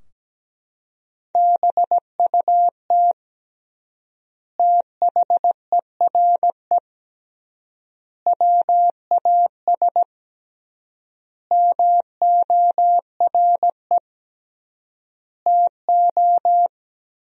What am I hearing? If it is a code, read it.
Morse code, heard as a keyed tone, about 17 wpm, 700 Hz. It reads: BUT THERE WAS MORE TO
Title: 3x Word-Spacing - Sentences from Top 100 Words - 17wpm